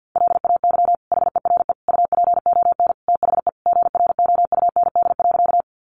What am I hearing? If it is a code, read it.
Morse code, heard as a keyed tone, about 50 wpm, 700 Hz. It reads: LAY HERE UPON THE GROUND.
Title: Sentences from Top 1100 Words 50wpm